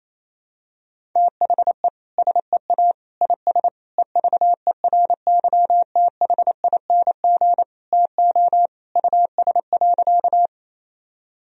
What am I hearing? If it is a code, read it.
Morse code, heard as a keyed tone, about 28 wpm, 700 Hz. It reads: THE SEA IS EVERYTHING TO US.
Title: Sentences from Top 1000 Words - Review Part 1 - 28wpm